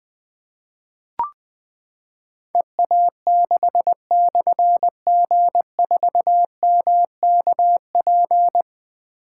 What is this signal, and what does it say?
Morse code, heard as a keyed tone, about 20 wpm, 700 Hz. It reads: EA6/G4MKP
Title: Difficult Call Signs 20wpm